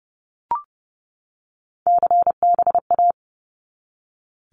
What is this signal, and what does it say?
Morse code, heard as a keyed tone, about 30 wpm, 700 Hz. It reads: CBA
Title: Abbreviations 30wpm